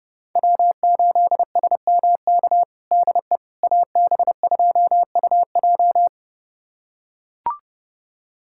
Morse code, heard as a keyed tone, about 30 wpm, 700 Hz. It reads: W8SMK DE AB2UJ
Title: USA Callsign DE Callsign - 5 Characters - 30wpm